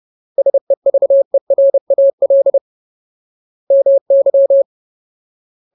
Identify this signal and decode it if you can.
Morse code, heard as a keyed tone, about 30 wpm, 550 Hz. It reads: SEVERAL MY
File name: Ultimate Word List / 30WPM Ultimate